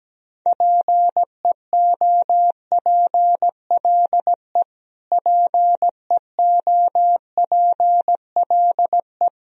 Morse code, heard as a keyed tone, about 17 wpm, 700 Hz. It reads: PEOPLE PEOPLE
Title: Top 100 Words Repeated 17wpm